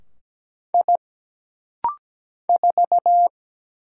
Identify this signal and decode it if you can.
Morse code, heard as a keyed tone, about 17 wpm, 700 Hz. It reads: I 4
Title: Single Letter-Number - Rapid-Fire 17wpm